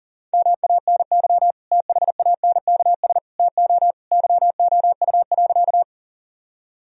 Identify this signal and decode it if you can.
Morse code, heard as a keyed tone, about 40 wpm, 700 Hz. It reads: MANY THANKS TO YOU.
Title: Sentences from Top 1000 Words - Review Part 3 - 40wpm